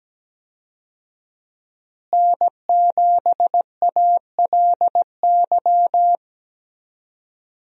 Morse code, heard as a keyed tone, about 17 wpm, 700 Hz. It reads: N7ALY
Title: USA Call Signs - 5 Characters - Repeated 2x - 17wpm